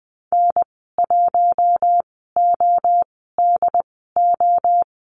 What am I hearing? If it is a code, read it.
Morse code, heard as a keyed tone, about 20 wpm, spaced as for 14 wpm, 700 Hz. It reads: N1ODO
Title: 069 Lesson 12g - Letter d - Call Signs 20wpm